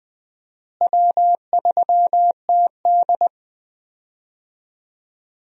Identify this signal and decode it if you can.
Morse code, heard as a keyed tone, about 20 wpm, 700 Hz. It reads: W3TD
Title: USA Call Signs - 4 Characters - Repeated 3x - 20wpm